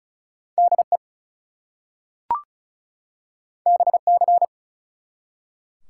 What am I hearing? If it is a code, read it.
Morse code, heard as a keyed tone, about 36 wpm, 700 Hz. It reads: DE BC
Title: US States and Canadian Provinces 35wpm